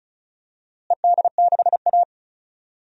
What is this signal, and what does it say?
Morse code, heard as a keyed tone, about 36 wpm, 700 Hz. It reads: ED6A
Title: DXCC-100 Call Signs - 4 Characters - Repeated 3x - 35wpm